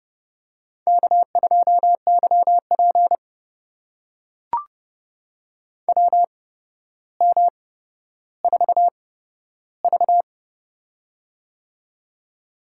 Morse code, heard as a keyed tone, about 30 wpm, spaced as for 13 wpm, 700 Hz. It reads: K2YP WM4V
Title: USA Call Signs - 4 Characters - 8x Character Spacing - 30wpm